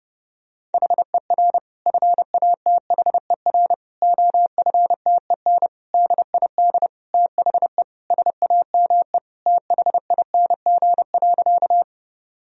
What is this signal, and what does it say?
Morse code, heard as a keyed tone, about 30 wpm, 700 Hz. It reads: HER FATHER OFTEN DID THE SAME THING.
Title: Sentences from Top 1000 Words - Review Part 1 - 30wpm